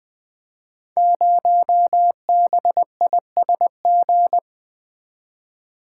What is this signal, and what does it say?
Morse code, heard as a keyed tone, about 20 wpm, 700 Hz. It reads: 0BISG
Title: Code Groups - 5 Letter:Numbers - 20wpm